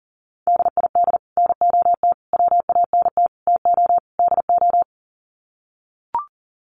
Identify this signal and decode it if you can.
Morse code, heard as a keyed tone, about 40 wpm, 700 Hz. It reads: DID NOT WANT TO DO
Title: Sets of 5 Words Encore 40wpm